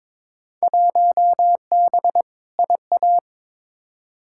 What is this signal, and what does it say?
Morse code, heard as a keyed tone, about 22 wpm, 700 Hz. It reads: 1B IA
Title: Field Day Exchanges Encore - 22wpm